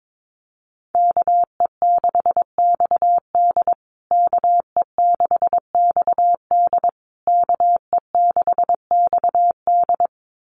Morse code, heard as a keyed tone, about 22 wpm, 700 Hz. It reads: KE6XD KE6XD KE6XD
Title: USA Call Signs Repeated 3x 22wpm